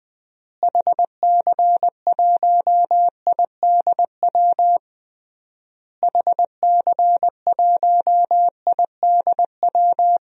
Morse code, heard as a keyed tone, about 20 wpm, 700 Hz. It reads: HC1IDW HC1IDW
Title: DXCC-100 Call Signs - 6 Characters - Repeated 3x - 20wpm